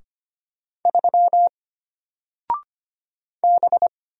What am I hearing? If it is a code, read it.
Morse code, heard as a keyed tone, about 25 wpm, 700 Hz. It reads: 3 B
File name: Single-Letter-Number-25wpm